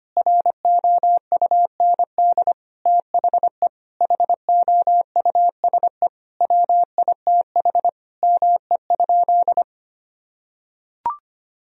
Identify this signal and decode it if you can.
Morse code, heard as a keyed tone, about 25 wpm, 700 Hz. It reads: ROUND THE HOUSE WITH ME?
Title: Sentences from Top 1000 Words - Review Part 1 - 25wpm